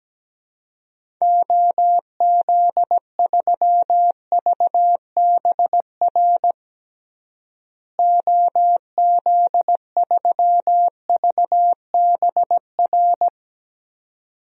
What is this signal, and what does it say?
Morse code, heard as a keyed tone, about 17 wpm, 700 Hz. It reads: OZ3VBR OZ3VBR
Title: DXCC-100 Call Signs - 6 Characters - Repeated 2x - 17wpm